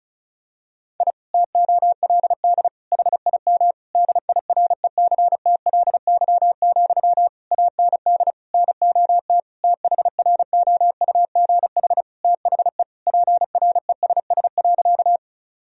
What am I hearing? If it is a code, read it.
Morse code, heard as a keyed tone, about 34 wpm, 700 Hz. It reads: I TOLD HIM DIRECTLY, AND NOT THROUGH THE PRESS.
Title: Sentences from Top 1000 Words - Review Part 1 - 35wpm